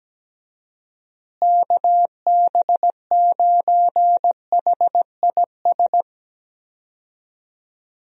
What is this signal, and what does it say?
Morse code, heard as a keyed tone, about 17 wpm, 700 Hz. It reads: KB9HIS
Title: USA Call Signs - 6 Characters - Repeated 2x - 17wpm